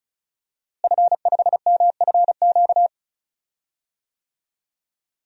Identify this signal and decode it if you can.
Morse code, heard as a keyed tone, about 34 wpm, 700 Hz. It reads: F5MFQ
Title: DXCC-100 Call Signs - 5 Characters - Repeated 2x - 35wpm